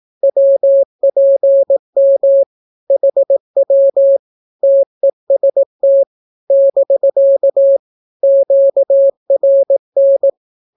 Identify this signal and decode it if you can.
Morse code, heard as a keyed tone, about 18 wpm, 550 Hz. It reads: WPM HW TEST <BK> QRN
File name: naukapodstaw / skroty / skroty